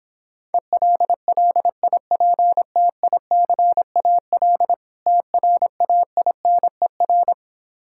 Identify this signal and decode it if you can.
Morse code, heard as a keyed tone, about 26 wpm, 700 Hz. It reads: ELLIPTICAL TRAINER